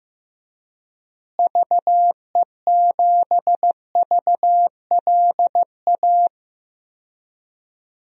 Morse code, heard as a keyed tone, about 15 wpm, 700 Hz. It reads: VE7VLA
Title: DXCC-100 Call Signs - 6 Characters - Repeated 2x - 15wpm